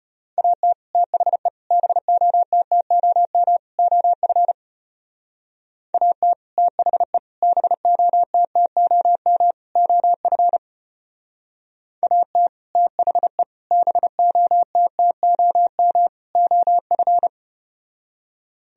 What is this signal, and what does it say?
Morse code, heard as a keyed tone, about 33 wpm, 700 Hz. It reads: AT THE BOTTOM OF AT THE BOTTOM OF AT THE BOTTOM OF